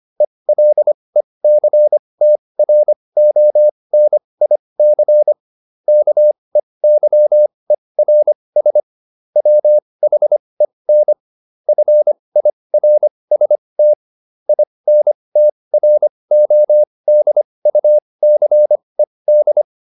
Morse code, heard as a keyed tone, about 25 wpm, spaced as for 20 wpm, 600 Hz. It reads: ELECTRONIC KEYERS WHEN FIRST INTRODUCED